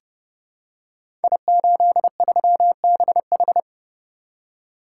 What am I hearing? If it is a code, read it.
Morse code, heard as a keyed tone, about 30 wpm, 700 Hz. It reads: I83BH
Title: Code Groups - 5 Letter:Numbers - 30wpm